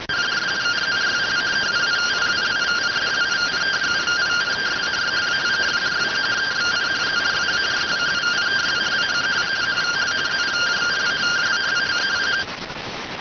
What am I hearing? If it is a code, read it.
Two-tone teleprinter signal, about 170 Hz shift.